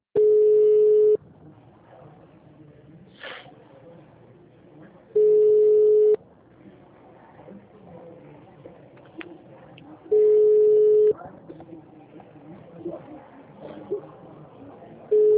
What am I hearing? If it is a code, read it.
Ringback tone (Europe, 425 Hz), with speech.
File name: out- -2013